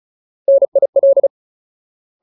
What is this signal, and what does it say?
Morse code, heard as a keyed tone, about 34 wpm, 550 Hz. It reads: NIL
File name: Ultimate Word List / 35WPM Ultimate